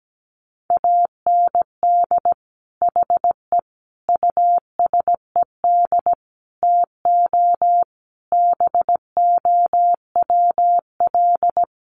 Morse code, heard as a keyed tone, about 17 wpm, 700 Hz. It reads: AND HE USED TO BOWL
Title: Sentences from Top 1200 Words 17wpm